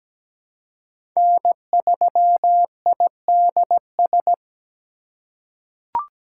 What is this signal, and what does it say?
Morse code, heard as a keyed tone, about 17 wpm, 700 Hz. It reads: N3IDS
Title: USA Call Signs - 5 Characters - 2x Character Spacing - 17wpm